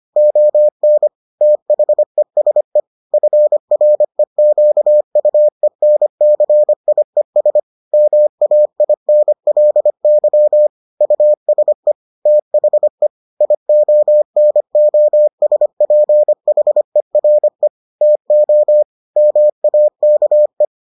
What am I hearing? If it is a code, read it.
Morse code, heard as a keyed tone, about 25 wpm, 600 Hz. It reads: ON THESE FREQUENCIES MAINLY USE THE IONOSPHERE TO MAKE